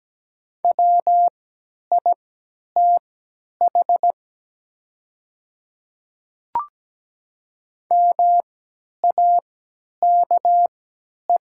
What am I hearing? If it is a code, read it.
Morse code, heard as a keyed tone, about 17 wpm, spaced as for 10 wpm, 700 Hz. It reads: WITH MAKE
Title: 3x Character Spacing - Top 100 Words - 17wpm